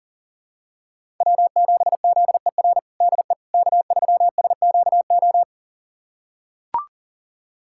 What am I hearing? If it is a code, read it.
Morse code, heard as a keyed tone, about 37 wpm, 700 Hz. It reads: W7ZER DE K3SQO
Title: USA Callsign DE Callsign - 5 Characters - 40wpm